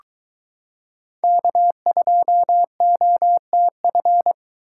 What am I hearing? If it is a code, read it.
Morse code, heard as a keyed tone, about 23 wpm, 700 Hz. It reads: K2OTF